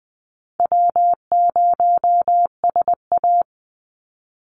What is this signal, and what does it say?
Morse code, heard as a keyed tone, about 20 wpm, 700 Hz. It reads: W0SA
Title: Easy Call Signs - 20wpm